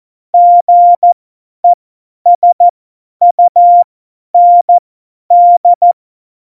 Morse code, heard as a keyed tone, about 14 wpm, spaced as for 10 wpm, 700 Hz. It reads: GESUND